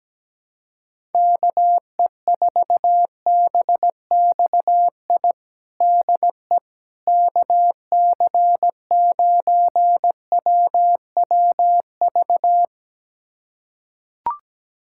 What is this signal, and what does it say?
Morse code, heard as a keyed tone, about 17 wpm, 700 Hz. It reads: KE4BXI DE KC9WWV
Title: USA Callsign DE Callsign 17wpm